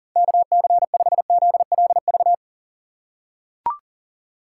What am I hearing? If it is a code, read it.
Morse code, heard as a keyed tone, about 40 wpm, 700 Hz. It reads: KC5ZLV
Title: USA Call Signs - 3x Character Spacing - 40wpm